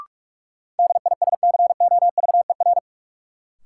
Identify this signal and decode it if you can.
Morse code, heard as a keyed tone, about 47 wpm, 700 Hz. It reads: DISCOVER
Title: Sentences from Top 1100 Words - New Word Drill - 45wpm